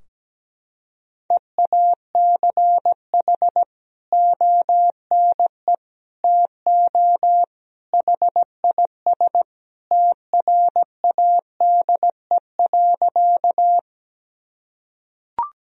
Morse code, heard as a keyed tone, about 17 wpm, 700 Hz. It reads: EACH ONE TO HIS TRADE.
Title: Sentences from Top 1000 Words - Review Part 4 - 17wpm